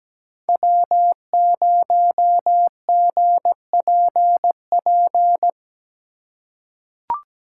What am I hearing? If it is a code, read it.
Morse code, heard as a keyed tone, about 17 wpm, 700 Hz. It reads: W0GPP
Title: USA Call Signs - 5 Characters - 2x Character Spacing - 17wpm